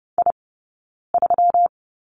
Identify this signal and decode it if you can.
Morse code, heard as a keyed tone, about 30 wpm, 700 Hz. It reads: I 3
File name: Lesson 20d - Number 3 - Two Characters 30wpm